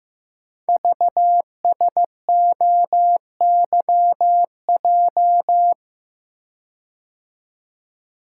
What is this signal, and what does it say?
Morse code, heard as a keyed tone, about 15 wpm, 700 Hz. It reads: VSOYJ
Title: Code Groups - 5 Letters - 15wpm